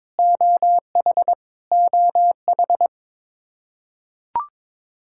Morse code, heard as a keyed tone, about 22 wpm, 700 Hz. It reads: OH OH
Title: K1USN SST QSO Practice 22wpm